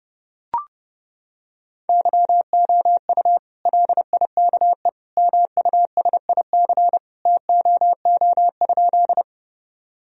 Morse code, heard as a keyed tone, about 30 wpm, 700 Hz. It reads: YOU LIKE MUSIC TOO?
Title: Sentences from Top 1000 Words - Review Part 1 - 30wpm